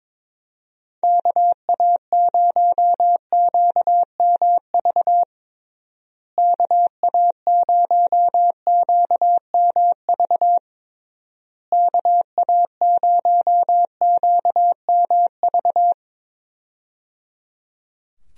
Morse code, heard as a keyed tone, about 22 wpm, 700 Hz. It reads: KA0QMV KA0QMV KA0QMV
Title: USA Call Signs - 6 Characters - Repeated 3x - 22wpm